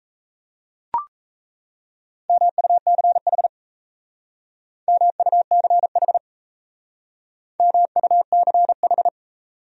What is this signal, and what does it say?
Morse code, heard as a keyed tone, about 34 wpm, 700 Hz. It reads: MUCH MUCH MUCH
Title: Speed-Racing Top 200 Words 28wpm